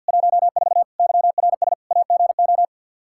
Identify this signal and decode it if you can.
Morse code, heard as a keyed tone, about 52 wpm, 700 Hz. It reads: 14 YRS AGO
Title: Practice QSO - Exchange 3 - More Info - 50wpm